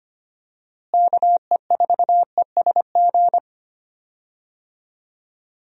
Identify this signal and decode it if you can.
Morse code, heard as a keyed tone, about 25 wpm, 700 Hz. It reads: KE4ESG
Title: USA Call Signs - 6 Characters - Repeated 3x - 25wpm